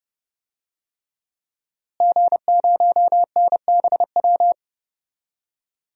Morse code, heard as a keyed tone, about 32 wpm, 700 Hz. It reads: G0NBW
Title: DXCC-100 Call Signs - 5 Characters - Repeated 3x - 30wpm